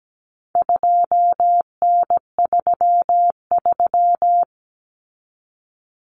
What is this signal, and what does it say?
Morse code, heard as a keyed tone, about 17 wpm, 700 Hz. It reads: 2N33
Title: Common Cut Numbers - 4-Digit 17wpm